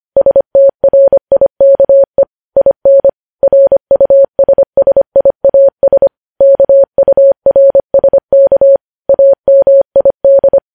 Morse code, heard as a keyed tone, about 25 wpm, 550 Hz. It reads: STRIKE IN RUSSIAS KURSK AMID
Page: News Headlines in Morse Code at 25 WPM